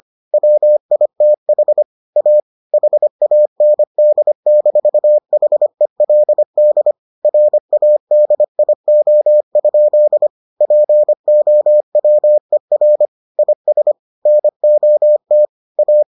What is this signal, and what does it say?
Morse code, heard as a keyed tone, about 25 wpm, 600 Hz. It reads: WITH A HAND-HELD RADIO? POWER IS NOT A